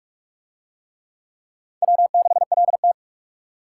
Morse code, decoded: W6LT

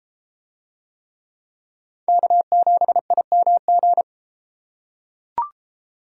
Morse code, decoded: K7IMG

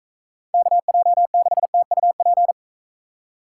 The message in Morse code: KJ6TUP

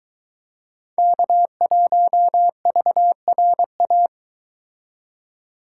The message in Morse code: K1VRA

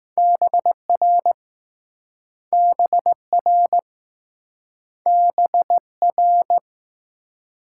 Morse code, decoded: BR BR B R

